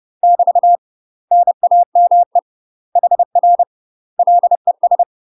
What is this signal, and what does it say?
Morse code, heard as a keyed tone, about 30 wpm, 700 Hz. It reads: <BT> NAME HR LES
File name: UK MASTER QSOs TRAINING / 30WPM / UK MASTER QSOs 0044